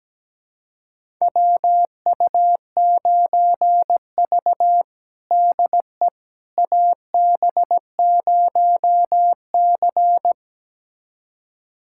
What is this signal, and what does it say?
Morse code, heard as a keyed tone, about 17 wpm, 700 Hz. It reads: WU9V DE AB0C
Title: USA Callsign DE Callsign - 4 Characters - 17wpm